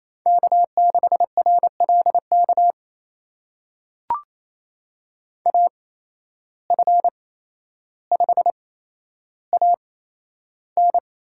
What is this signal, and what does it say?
Morse code, heard as a keyed tone, about 28 wpm, spaced as for 12 wpm, 700 Hz. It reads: K6RLK AF5AN
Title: USA Call Signs - 5 Characters - 8x Character Spacing - 28wpm